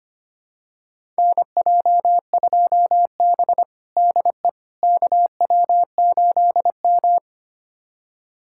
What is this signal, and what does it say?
Morse code, heard as a keyed tone, about 25 wpm, 700 Hz. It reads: NJ2B DE KW8M
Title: USA Callsign DE Callsign - 4 Characters - 25wpm